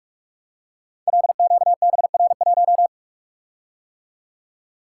Morse code, heard as a keyed tone, about 44 wpm, 700 Hz. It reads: LQBR1
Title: Code Groups - 5 Letter:Numbers - 45wpm